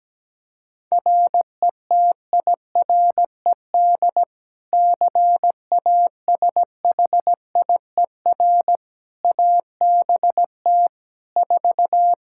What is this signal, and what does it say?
Morse code, heard as a keyed tone, about 17 wpm, 700 Hz. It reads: RETIRED CASHIER ABT 4
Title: Practice QSO - Exchange 3 - More Info - 17wpm